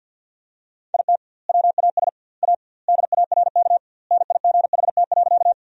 Morse code, heard as a keyed tone, about 51 wpm, 700 Hz. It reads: IT WAS A DARK NIGHT.
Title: 2x Word-Spacing - Sentences from Top 500 Words - 50wpm